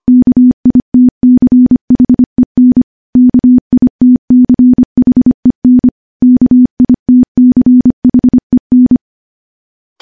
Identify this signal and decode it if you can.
Morse code, heard as a keyed tone, about 25 wpm, 250 Hz. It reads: KITCHEN KITCHEN KITCHEN
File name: morse / morse-training-25WPM